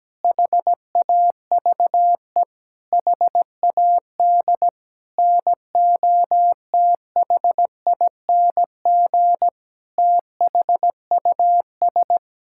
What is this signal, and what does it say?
Morse code, heard as a keyed tone, about 17 wpm, 700 Hz. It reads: HAVE HAD NOTHING THUS
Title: Sentences from Top 1000 Words - Review Part 5 - 17wpm